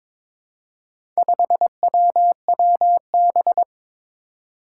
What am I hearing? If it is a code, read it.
Morse code, heard as a keyed tone, about 21 wpm, 700 Hz. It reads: 5WWB
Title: Four Challenging Letter-Number Combinations 22wpm